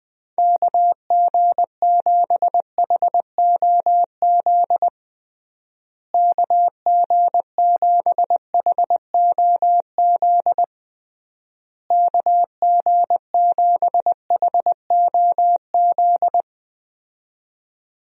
Morse code, heard as a keyed tone, about 21 wpm, 700 Hz. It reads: KG7HOZ KG7HOZ KG7HOZ
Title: USA Call Signs - 6 Characters - Repeated 3x - 20wpm